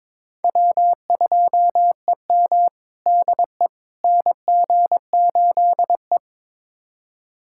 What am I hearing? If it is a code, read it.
Morse code, heard as a keyed tone, about 22 wpm, 700 Hz. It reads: W2EM DE NG8E
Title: USA Callsign DE Callsign - 4 Characters - 22wpm